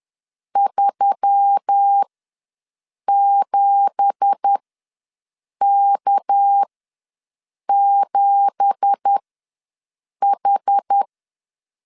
Morse code, heard as a keyed tone, about 10 wpm, spaced as for 6 wpm, 800 Hz. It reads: 37K7H
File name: LEZIONI / 9 2867 ws 10wpm 4215